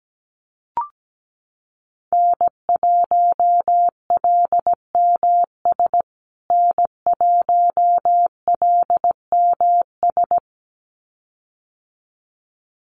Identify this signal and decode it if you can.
Morse code, heard as a keyed tone, about 17 wpm, 700 Hz. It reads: N1LMS N1LMS
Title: USA Call Signs Repeated 2x 17wpm